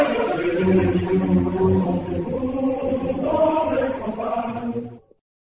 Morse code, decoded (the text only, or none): none